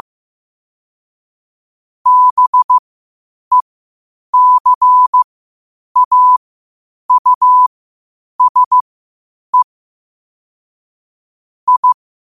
Morse code, heard as a keyed tone, about 15 wpm, spaced as for 8 wpm, 1000 Hz. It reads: BECAUSE I
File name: qotdfiles / 8wpm